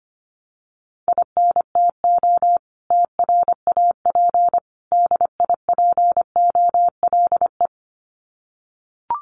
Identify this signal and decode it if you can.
Morse code, heard as a keyed tone, about 25 wpm, 700 Hz. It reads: INTO TRAP DIPOLE
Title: Common QSO Phrases 25wpm